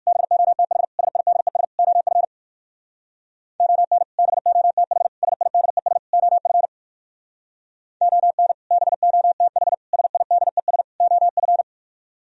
Morse code, heard as a keyed tone, about 53 wpm, 700 Hz. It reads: BOTH SIDES OF ON BOTH SIDES OF ON BOTH SIDES OF
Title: Speed-Racing - Sets of 4 Words 45wpm